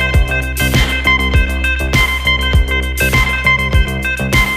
Emergency Alert System (SAME) tones, heard as harsh buzzing bursts. It is heard from 0.9 to 3.6 s.